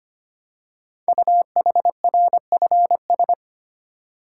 Morse code, heard as a keyed tone, about 25 wpm, 700 Hz. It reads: UHRFS
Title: Code Groups - 5 Letters - 25wpm